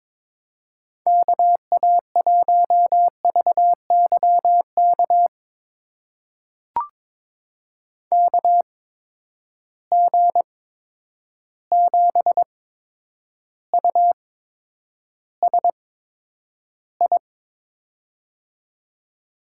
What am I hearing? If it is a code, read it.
Morse code, heard as a keyed tone, about 22 wpm, spaced as for 9 wpm, 700 Hz. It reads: KA1VYK KG7USI
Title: USA Call Signs - 6 Characters - 8x Character Spacing - 22wpm